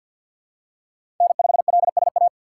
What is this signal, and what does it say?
Morse code, heard as a keyed tone, about 48 wpm, 700 Hz. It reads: N5LSA
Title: USA Call Signs - 2x Character Spacing - 50wpm